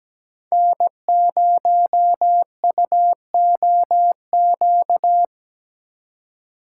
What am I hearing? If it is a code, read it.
Morse code, heard as a keyed tone, about 17 wpm, 700 Hz. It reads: N0UOQ
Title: USA Call Signs - 5 Characters - 3x Character Spacing - 17wpm